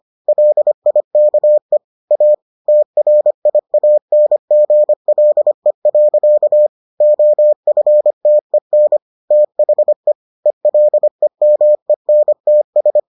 Morse code, heard as a keyed tone, about 25 wpm, 600 Hz. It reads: LIKE A TRIANGLE. OFTEN THE ELEMENTS